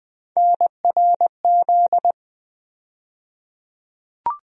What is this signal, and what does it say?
Morse code, heard as a keyed tone, about 21 wpm, 700 Hz. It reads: NRZ
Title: USA Call Sign Postfixes - 3 Letters 20wpm